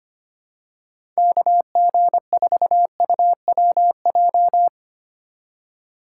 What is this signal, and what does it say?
Morse code, heard as a keyed tone, about 25 wpm, 700 Hz. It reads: KG4UWJ
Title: USA Call Signs - 6 Characters - 3x Character Spacing - 25wpm